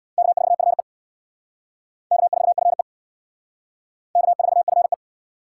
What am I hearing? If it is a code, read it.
Morse code, heard as a keyed tone, about 63 wpm, 700 Hz. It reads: K4FE K4FE K4FE